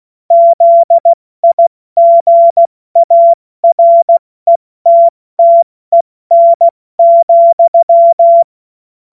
Morse code, heard as a keyed tone, about 16 wpm, spaced as for 14 wpm, 680 Hz. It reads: ZIGARETTEN,
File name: Morsekurs / Witze / Witz 23. Mai 2023 70 F80